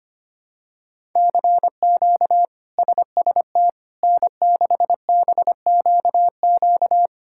Morse code, heard as a keyed tone, about 25 wpm, 700 Hz. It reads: CQ SST N6BQQ
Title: K1USN SST QSO Practice - 25wpm